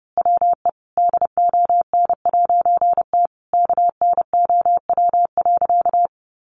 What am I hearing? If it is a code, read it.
Morse code, heard as a keyed tone, about 30 wpm, 700 Hz. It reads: WE DON'T KNOW.
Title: A Sound Of Thunder 30wpm